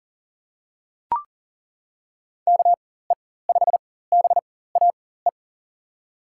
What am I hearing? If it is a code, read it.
Morse code, heard as a keyed tone, about 40 wpm, spaced as for 19 wpm, 700 Hz. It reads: KE5BAE
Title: USA Call Signs - 6 Characters - 4x Character Spacing - 40wpm